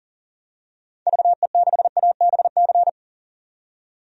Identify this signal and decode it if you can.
Morse code, heard as a keyed tone, about 37 wpm, 700 Hz. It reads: VE6ABC
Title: DXCC-100 Call Signs - 6 Characters - 40wpm